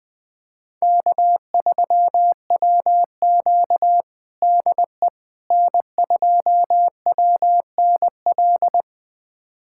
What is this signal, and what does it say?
Morse code, heard as a keyed tone, about 21 wpm, 700 Hz. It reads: K3WQ DE N2WNL
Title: USA Callsign DE Callsign 20wpm